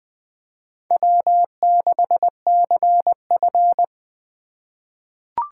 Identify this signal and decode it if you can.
Morse code, heard as a keyed tone, about 20 wpm, 700 Hz. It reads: W6CF
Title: USA Call Signs - 4 Characters - 4x Character Spacing - 20wpm